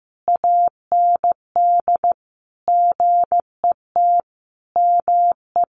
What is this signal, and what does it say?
Morse code, heard as a keyed tone, about 15 wpm, 700 Hz. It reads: AND GET ME
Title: Sentences from Top 1500 Words 15wpm